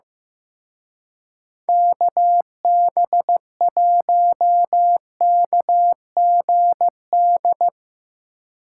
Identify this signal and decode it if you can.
Morse code, heard as a keyed tone, about 17 wpm, 700 Hz. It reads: KB1KGD
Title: Speed Racing - USA 2x3 Call Signs 15wpm